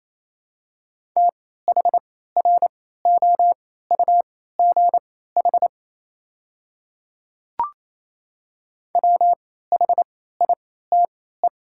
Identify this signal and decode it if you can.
Morse code, heard as a keyed tone, about 27 wpm, spaced as for 16 wpm, 690 Hz. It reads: THROUGH WHITE
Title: 3x Character Spacing - Top 300 Words - 28wpm